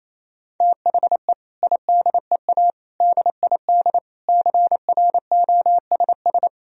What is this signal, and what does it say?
Morse code, heard as a keyed tone, about 28 wpm, 700 Hz. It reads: THE IDEA DID CROSS
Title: Sentences from Top 1000 Words - Review Part 1 - 28wpm